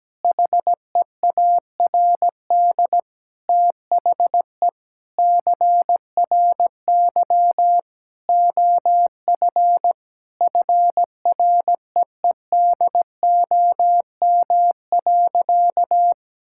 Morse code, heard as a keyed tone, about 17 wpm, 700 Hz. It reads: HEARD THE CRY OF FREEDOM.
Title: Sentences from Top 1000 Words - Review Part 5 - 17wpm